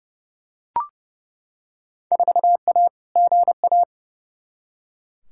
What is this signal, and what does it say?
Morse code, heard as a keyed tone, about 31 wpm, 700 Hz. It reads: 4A GA
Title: Field Day Exchanges - 30wpm